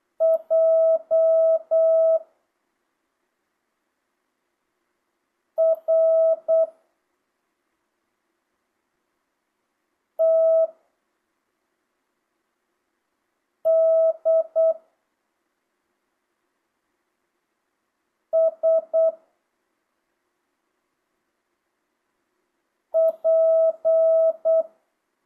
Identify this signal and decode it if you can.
Morse code, heard as a keyed tone, about 8 wpm, 650 Hz. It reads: J R T D S P
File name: jh8cbh / 2025cw